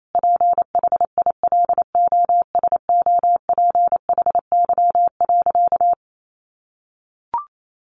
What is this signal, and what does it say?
Morse code, heard as a keyed tone, about 28 wpm, 700 Hz. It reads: PHILOSOPHY.
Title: Sentences from Top 900 Words 28wpm